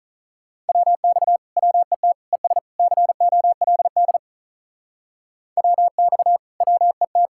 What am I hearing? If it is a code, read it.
Morse code, heard as a keyed tone, about 42 wpm, 700 Hz. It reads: WX WET ES COLD WX WET
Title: Speed-Racing Common QSO Phrases - 35wpm